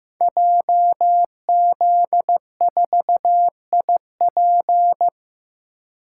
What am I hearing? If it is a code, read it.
Morse code, heard as a keyed tone, about 15 wpm, 700 Hz. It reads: JZ4IP
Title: Code Groups - 5 Letter:Numbers - 15wpm